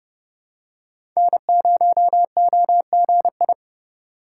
Morse code, heard as a keyed tone, about 30 wpm, 700 Hz. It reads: N0OGI
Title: USA Call Signs - 5 Characters - 3x Character Spacing - 30wpm